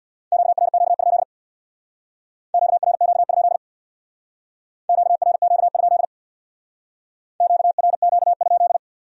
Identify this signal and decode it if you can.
Morse code, heard as a keyed tone, about 63 wpm, 700 Hz. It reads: QRQ? QRQ? QRQ? QRQ?